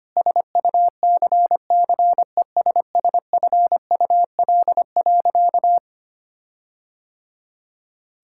Morse code, heard as a keyed tone, about 25 wpm, 700 Hz. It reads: SUCCESSFUL.
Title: Sentences from Top 1600 Words 25wpm